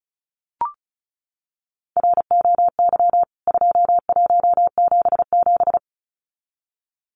Morse code, heard as a keyed tone, about 34 wpm, 700 Hz. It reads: ROY 2177
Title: CWT Exchange 35wpm